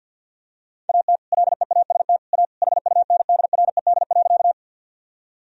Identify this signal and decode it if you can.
Morse code, heard as a keyed tone, about 50 wpm, 700 Hz. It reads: AT LEAST A HUNDRED.
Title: Sentences from Top 1000 Words - Review Part 2 - 50wpm